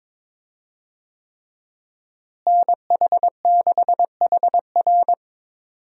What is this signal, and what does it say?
Morse code, heard as a keyed tone, about 22 wpm, 700 Hz. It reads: NH6HR
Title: USA Call Signs - 5 Characters - Repeated 3x - 22wpm